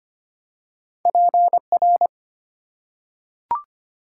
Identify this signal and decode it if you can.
Morse code, heard as a keyed tone, about 25 wpm, 700 Hz. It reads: PR
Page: morse-code-ninja-practice-25wpm